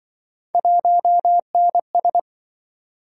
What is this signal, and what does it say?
Morse code, heard as a keyed tone, about 27 wpm, 700 Hz. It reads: 1NS